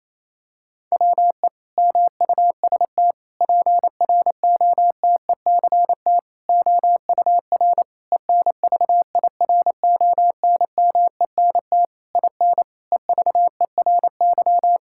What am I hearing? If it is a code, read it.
Morse code, heard as a keyed tone, about 28 wpm, 700 Hz. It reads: WE MUST PROTECT OUR ENVIRONMENT IN EVERY
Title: Sentences from Top 1000 Words - Review Part 5 - 28wpm